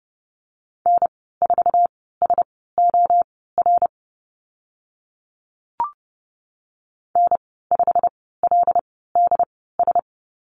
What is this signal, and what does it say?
Morse code, heard as a keyed tone, about 30 wpm, spaced as for 17 wpm, 700 Hz. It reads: N4SOR N5LDS
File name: Lesson 15g - Number 5 - Call Signs 30wpm